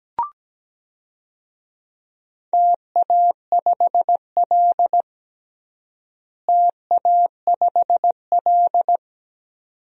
Morse code, heard as a keyed tone, about 17 wpm, 700 Hz. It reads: TA5L TA5L
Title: DXCC-100 Call Signs - 4 Characters - Repeated 2x - 17wpm